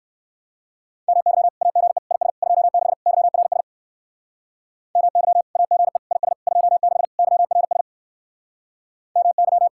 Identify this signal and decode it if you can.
Morse code, heard as a keyed tone, about 59 wpm, 700 Hz. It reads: MY AGE IS 26 YRS MY AGE IS 26 YRS MY